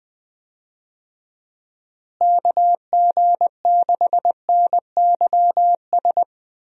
Morse code, heard as a keyed tone, about 21 wpm, 700 Hz. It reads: KG6NYS
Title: USA Call Signs - 6 Characters - Repeated 3x - 20wpm